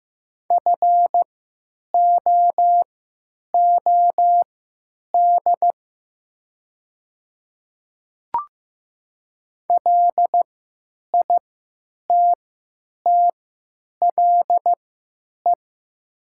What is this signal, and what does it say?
Morse code, heard as a keyed tone, about 15 wpm, spaced as for 9 wpm, 700 Hz. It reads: FOOD LITTLE
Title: 3x Character Spacing - Top 300 Words - 15wpm